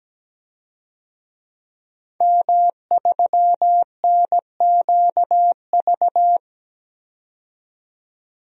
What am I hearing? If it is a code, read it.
Morse code, heard as a keyed tone, about 17 wpm, 700 Hz. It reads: M3NQV
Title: DXCC-100 Call Signs - 5 Characters - 17wpm